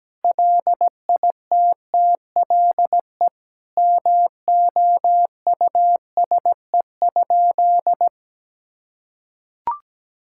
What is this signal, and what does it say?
Morse code, heard as a keyed tone, about 17 wpm, 700 Hz. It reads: LITTLE MOUSE?
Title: Sentences from Top 1000 Words - Review Part 5 - 17wpm